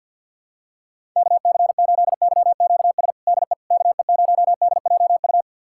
Morse code, heard as a keyed tone, about 51 wpm, 700 Hz. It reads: KC8YQS DE KE0DJU
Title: USA Callsign DE Callsign 50wpm